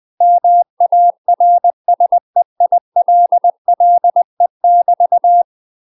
Morse code, heard as a keyed tone, about 20 wpm, 700 Hz. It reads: MARSEILLE <BT>
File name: UK MASTER QSOs TRAINING/20WPM